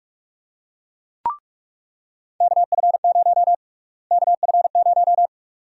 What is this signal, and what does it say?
Morse code, heard as a keyed tone, about 46 wpm, 690 Hz. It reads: KF0 KF0
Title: USA Call Sign Prefixes - Repeated 2x - 45wpm